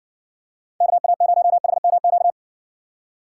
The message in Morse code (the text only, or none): KA0HMQ